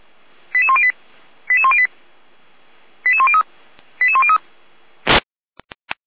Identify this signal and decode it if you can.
Five-tone selective call (ZVEI), heard as quick runs of five tones, tones of about 70 ms. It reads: 88108 88108 88103 88103
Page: /455/SDR/ZVEI/88469/2025.07.22/